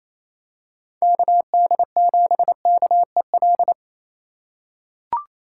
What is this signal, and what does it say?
Morse code, heard as a keyed tone, about 27 wpm, 700 Hz. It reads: KD7KEL
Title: USA Call Signs - 6 Characters - 4x Character Spacing - 28wpm